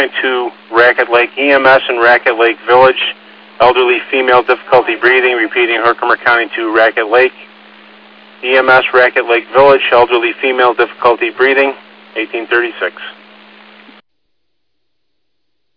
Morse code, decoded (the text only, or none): none